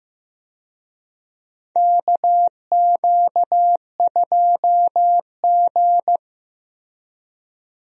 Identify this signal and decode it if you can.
Morse code, heard as a keyed tone, about 15 wpm, 700 Hz. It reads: KQ2G
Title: USA Call Signs - 4 Characters - 15wpm